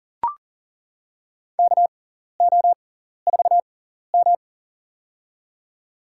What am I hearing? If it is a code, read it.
Morse code, heard as a keyed tone, about 39 wpm, spaced as for 14 wpm, 700 Hz. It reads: KO4M
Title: USA Call Signs - 4 Characters - 6x Character Spacing - 40wpm